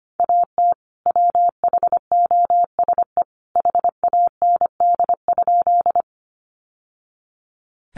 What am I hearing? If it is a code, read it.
Morse code, heard as a keyed tone, about 25 wpm, 700 Hz. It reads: AT WHOSE HAND?
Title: Sentences from Top 1200 Words 25wpm